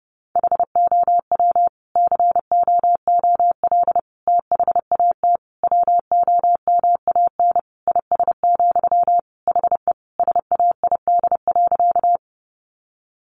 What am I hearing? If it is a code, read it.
Morse code, heard as a keyed tone, about 30 wpm, 700 Hz. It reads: HOW COOL THAT WOMAN IS, HE SAID.
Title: Sentences from Top 1200 Words 30wpm